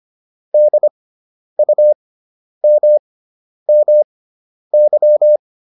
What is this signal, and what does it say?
Morse code, heard as a keyed tone, about 25 wpm, spaced as for 10 wpm, 600 Hz. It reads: DUMMY